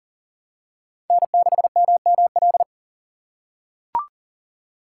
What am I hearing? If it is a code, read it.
Morse code, heard as a keyed tone, about 40 wpm, 700 Hz. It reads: N6MML
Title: USA Call Signs - 5 Characters - 4x Character Spacing - 40wpm